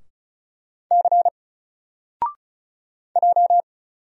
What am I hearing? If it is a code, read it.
Morse code, heard as a keyed tone, about 36 wpm, 700 Hz. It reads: C J
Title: Single Character (Letter:Number:Punctuation:Prosign) Alt 35wpm